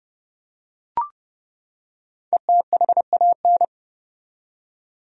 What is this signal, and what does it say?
Morse code, heard as a keyed tone, about 30 wpm, 700 Hz. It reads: ETHAN
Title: USA Common Names - 30wpm